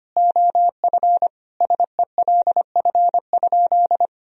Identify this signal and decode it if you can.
Morse code, heard as a keyed tone, about 25 wpm, 700 Hz. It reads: OF SELF?
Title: Sentences from Top 1000 Words - Review Part 1 - 25wpm